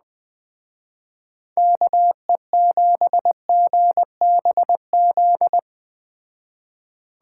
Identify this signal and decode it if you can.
Morse code, heard as a keyed tone, about 20 wpm, 700 Hz. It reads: KE7GBZ
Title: Speed Racing - USA 2x3 Call Signs 17wpm